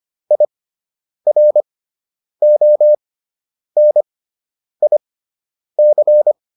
Morse code, heard as a keyed tone, about 25 wpm, spaced as for 9 wpm, 600 Hz. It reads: IRONIC